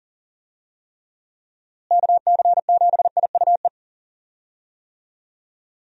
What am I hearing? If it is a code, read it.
Morse code, heard as a keyed tone, about 40 wpm, 700 Hz. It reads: KC7IUE
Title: USA Call Signs - 6 Characters - Repeated 3x - 40wpm